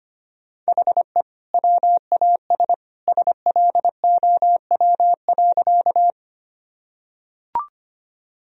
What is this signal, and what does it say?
Morse code, heard as a keyed tone, about 25 wpm, 700 Hz. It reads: HE WAS SLOW.
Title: Sentences from Top 1000 Words - Review Part 2 - 25wpm